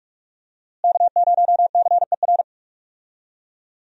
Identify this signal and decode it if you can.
Morse code, heard as a keyed tone, about 42 wpm, 700 Hz. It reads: K0CER
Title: USA Call Signs - 5 Characters - Repeated 2x - 45wpm